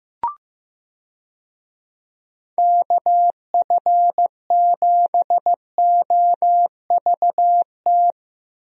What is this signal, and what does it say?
Morse code, heard as a keyed tone, about 15 wpm, 700 Hz. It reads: KF7OVT
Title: USA Call Signs - 6 Characters - Repeated 2x - 15wpm